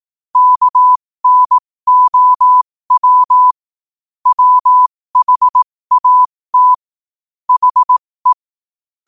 Morse code, heard as a keyed tone, about 18 wpm, spaced as for 15 wpm, 1000 Hz. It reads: KNOW WHAT HE
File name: qotdfiles / 15wpm